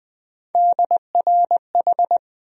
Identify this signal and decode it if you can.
Morse code, heard as a keyed tone, about 20 wpm, 700 Hz. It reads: DRH
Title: USA Call Sign Postfixes - 3 Letters 20wpm